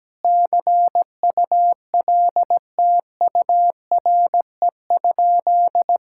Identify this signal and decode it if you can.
Morse code, heard as a keyed tone, about 17 wpm, 700 Hz. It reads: CULTURE?
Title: Sentences from Top 1000 Words - Review Part 6 - 17wpm